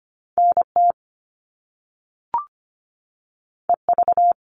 Morse code, heard as a keyed tone, about 25 wpm, 700 Hz. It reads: NT EV
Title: All Cut Numbers - 2-Digit 25wpm